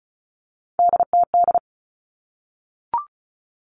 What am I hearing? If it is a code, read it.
Morse code, heard as a keyed tone, about 34 wpm, 700 Hz. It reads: DTD
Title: All Cut Numbers - 3-Digit 35wpm